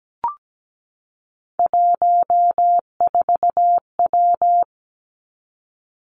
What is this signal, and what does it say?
Morse code, heard as a keyed tone, about 17 wpm, 700 Hz. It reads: 14W
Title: Three Challenging Letter-Number Combinations 17wpm